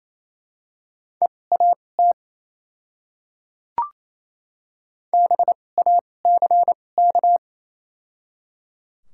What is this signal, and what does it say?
Morse code, heard as a keyed tone, about 28 wpm, spaced as for 20 wpm, 700 Hz. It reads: EAT BACK
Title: 2x Character Spacing - Top 300 Words - 28wpm